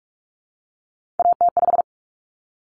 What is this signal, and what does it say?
Morse code, heard as a keyed tone, about 44 wpm, 700 Hz. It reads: AT5
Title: Common Cut Numbers - 3-Digit 45wpm